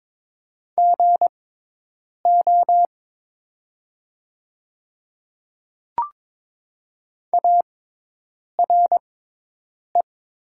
Morse code, heard as a keyed tone, about 23 wpm, spaced as for 8 wpm, 700 Hz. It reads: GO ARE
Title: 6x Character Spacing - Top 100 Words - 22wpm